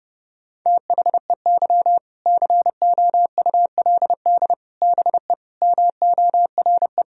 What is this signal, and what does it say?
Morse code, heard as a keyed tone, about 32 wpm, 700 Hz. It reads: THEY COULD BE MORE
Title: Sentences from Top 1000 Words - Review Part 4 - 30wpm